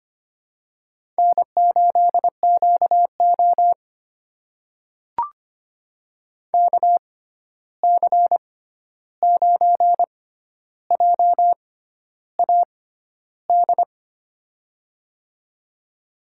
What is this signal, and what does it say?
Morse code, heard as a keyed tone, about 25 wpm, spaced as for 11 wpm, 700 Hz. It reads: N8QO KC9JAD
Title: USA Call Signs - 6x Character Spacing - 25wpm